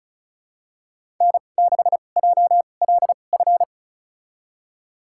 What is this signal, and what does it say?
Morse code, heard as a keyed tone, about 34 wpm, spaced as for 25 wpm, 700 Hz. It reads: N6JLF
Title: USA Call Signs - 5 Characters - 2x Character Spacing - 35wpm